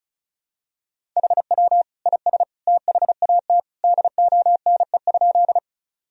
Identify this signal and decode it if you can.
Morse code, heard as a keyed tone, about 36 wpm, 700 Hz. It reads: HW IS THAT DONE?
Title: Sentences from Top 500 Words and Abbreviations 35wpm